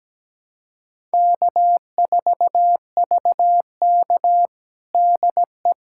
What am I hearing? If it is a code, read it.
Morse code, heard as a keyed tone, about 17 wpm, 700 Hz. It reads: K4VK DE
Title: USA Callsign DE Callsign - 4 Characters - 17wpm